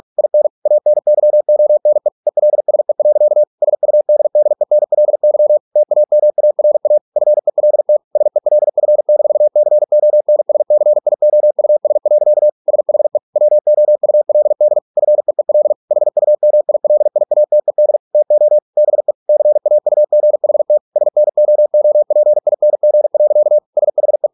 Morse code, heard as a keyed tone, about 46 wpm, 600 Hz. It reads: IN ANYONE ELSE. SUDDENLY TAMARA FELT SELF-CONSCIOUS. SHE WOULD FEEL HUMILIATED TO BE CAUGHT SNOOPING. SHE